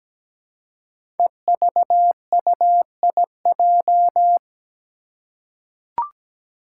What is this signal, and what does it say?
Morse code, heard as a keyed tone, about 17 wpm, 700 Hz. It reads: EVUIJ
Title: Five Challenging Letter-Number Combinations 17wpm